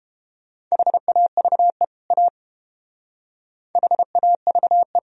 Morse code, heard as a keyed tone, about 30 wpm, 700 Hz. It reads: HAVE A HAVE